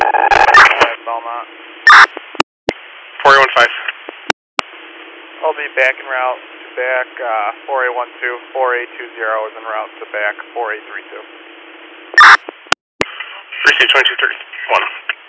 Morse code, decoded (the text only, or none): none